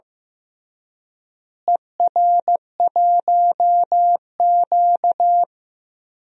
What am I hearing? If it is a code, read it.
Morse code, heard as a keyed tone, about 15 wpm, 700 Hz. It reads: ER1Q
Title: DXCC-100 Call Signs - 4 Characters - Repeated 3x - 15wpm